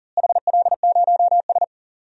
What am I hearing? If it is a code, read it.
Morse code, heard as a keyed tone, about 40 wpm, 700 Hz. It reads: HL0S
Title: DXCC-100 Call Signs - 4 Characters - Repeated 3x - 40wpm